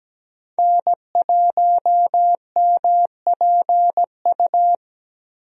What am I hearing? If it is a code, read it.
Morse code, heard as a keyed tone, about 17 wpm, 700 Hz. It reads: N1MPU
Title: USA Call Signs - 5 Characters - 17wpm